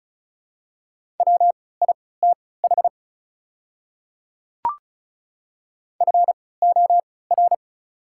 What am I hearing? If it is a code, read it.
Morse code, heard as a keyed tone, about 36 wpm, spaced as for 20 wpm, 700 Hz. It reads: WITH FOR
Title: 3x Character Spacing - Top 100 Words - 35wpm